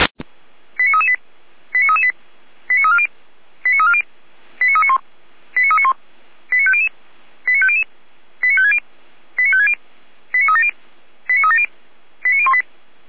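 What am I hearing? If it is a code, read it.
Five-tone selective call (ZVEI), heard as quick runs of five tones, tones of about 70 ms. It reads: 89338 89338 89340 89340 89391 89391 89500 89500 89560 89560 89370 89370 89018
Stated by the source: ZVEI five-tone call received on SDR, from the Feuerwehr Orlamünde fire brigade audio server.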